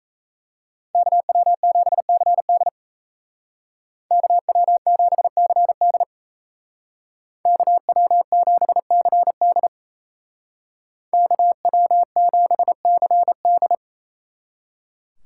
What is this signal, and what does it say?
Morse code, heard as a keyed tone, about 33 wpm, 700 Hz. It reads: KW7CD KW7CD KW7CD KW7CD
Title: Speed Racing - USA 2x2 Call Signs 28wpm